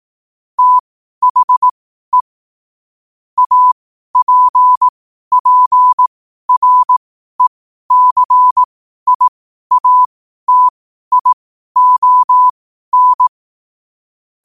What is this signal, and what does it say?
Morse code, heard as a keyed tone, about 18 wpm, spaced as for 12 wpm, 1000 Hz. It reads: THE APPRECIATION
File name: qotdfiles / 12wpm